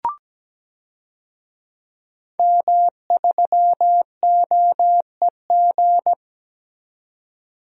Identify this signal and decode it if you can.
Morse code, heard as a keyed tone, about 17 wpm, 700 Hz. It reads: M3OEG
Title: DXCC-100 Call Signs - 5 Characters - 17wpm